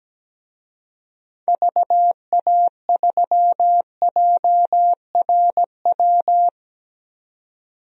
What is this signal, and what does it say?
Morse code, heard as a keyed tone, about 17 wpm, 700 Hz. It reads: VA3JRW
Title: DXCC-100 Call Signs - 6 Characters - 17wpm